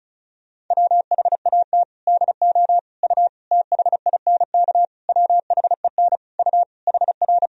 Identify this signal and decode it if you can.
Morse code, heard as a keyed tone, about 34 wpm, 700 Hz. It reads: WHAT DO U THINK WHEN U HR
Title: Sentences from Top 300 Words and Abbreviations 35wpm